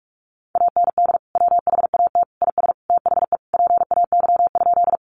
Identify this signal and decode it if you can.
Morse code, heard as a keyed tone, about 44 wpm, 700 Hz. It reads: AND WHAT IS THE PAY?
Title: Sentences from Top 700 Words 45wpm